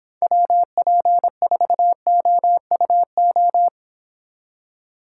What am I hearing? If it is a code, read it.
Morse code, heard as a keyed tone, about 30 wpm, 700 Hz. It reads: WP4OUO